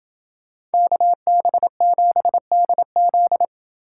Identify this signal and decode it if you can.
Morse code, heard as a keyed tone, about 28 wpm, 700 Hz. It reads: KB7DZ